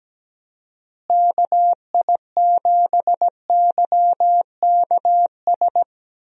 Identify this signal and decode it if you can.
Morse code, heard as a keyed tone, about 17 wpm, 700 Hz. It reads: KI7YKS
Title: USA Call Signs - 6 Characters - Repeated 3x - 17wpm